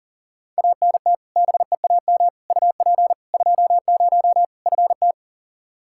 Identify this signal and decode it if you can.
Morse code, heard as a keyed tone, about 40 wpm, 700 Hz. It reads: ANT BEAM UP 20 FT
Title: Practice QSO - Exchange 2 - Info - 40wpm